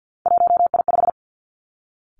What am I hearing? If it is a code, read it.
Morse code, heard as a keyed tone, about 50 wpm, 700 Hz. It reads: 1I5
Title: Three Challenging Letter-Number Combinations 50wpm